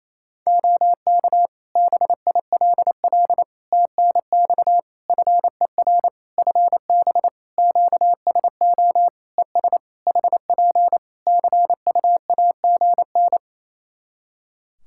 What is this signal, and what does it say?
Morse code, heard as a keyed tone, about 27 wpm, 700 Hz. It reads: OK BILL TNX FER FB QSO ES HP CUAGN
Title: Practice QSO - Exchange 4 - Ending - 28wpm